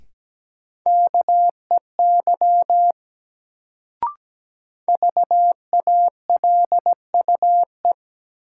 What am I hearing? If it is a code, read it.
Morse code, heard as a keyed tone, about 17 wpm, 700 Hz. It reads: KEY VALUE
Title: Sentences from Top 600 Words - New Word Drill - 17wpm